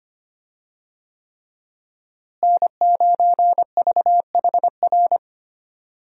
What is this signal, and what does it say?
Morse code, heard as a keyed tone, about 25 wpm, 700 Hz. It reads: N9VHR
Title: USA Call Signs - 5 Characters - Repeated 3x - 25wpm